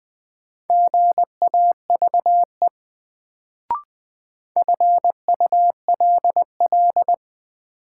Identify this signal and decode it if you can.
Morse code, heard as a keyed tone, about 20 wpm, 700 Hz. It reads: GAVE FULL
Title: Sentences from Top 500 Words - New Word Drill - 20wpm